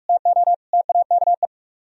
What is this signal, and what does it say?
Morse code, heard as a keyed tone, about 42 wpm, 700 Hz. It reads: TO TAKE